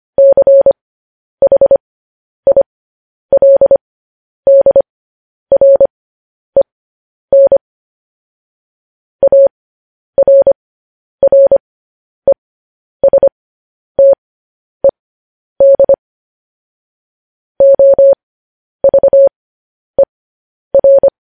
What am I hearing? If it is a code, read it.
Morse code, heard as a keyed tone, about 25 wpm, spaced as for 10 wpm, 550 Hz. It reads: CHILDREN ARRESTED OVER
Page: News Headlines in Morse Code at 10 WPM